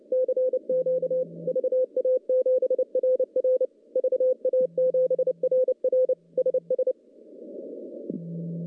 Morse code, heard as a keyed tone, about 30 wpm, 500 Hz. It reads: CQ VA7RR VA7RR SS
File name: contests / CW-SS-k5tr- -0900